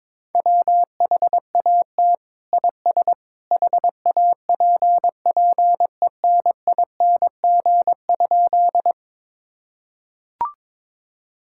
Morse code, heard as a keyed tone, about 22 wpm, 700 Hz. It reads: WHAT IS HAPPENING?
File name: practice / 22wpm / Sentences with 2-Word Phrases - 22wpm